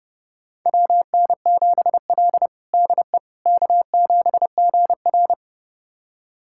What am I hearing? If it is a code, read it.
Morse code, heard as a keyed tone, about 32 wpm, 700 Hz. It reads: WN7L DE K7GR
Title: USA Callsign DE Callsign - 4 Characters - 30wpm